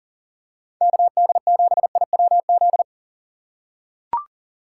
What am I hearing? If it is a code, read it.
Morse code, heard as a keyed tone, about 39 wpm, 700 Hz. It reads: KD7IWZ
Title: USA Call Signs - 6 Characters - 6x Character Spacing - 40wpm